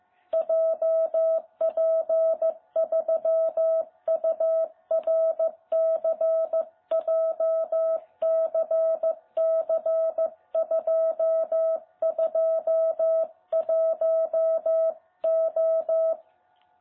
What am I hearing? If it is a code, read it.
Morse code, heard as a keyed tone, about 14 wpm, 660 Hz. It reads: JP3URC JCC221O
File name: jh8cbh / 2021cw